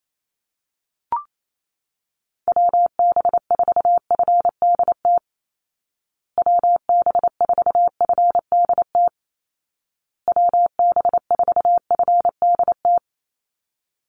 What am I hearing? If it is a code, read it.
Morse code, heard as a keyed tone, about 28 wpm, 700 Hz. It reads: WB4FDT WB4FDT WB4FDT
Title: Call Signs Repeated 2x - 28wpm